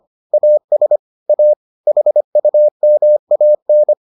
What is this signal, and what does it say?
Morse code, heard as a keyed tone, about 25 wpm, 600 Hz. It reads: AS A HUMAN